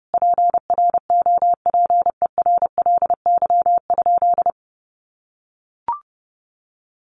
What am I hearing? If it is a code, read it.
Morse code, heard as a keyed tone, about 30 wpm, 700 Hz. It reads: PROPERLY?
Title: Sentences from Top 1100 Words 30wpm